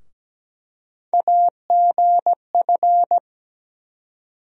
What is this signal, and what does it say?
Morse code, heard as a keyed tone, about 17 wpm, 700 Hz. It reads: AGF
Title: USA Call Sign Postfixes - 3 Letters 17wpm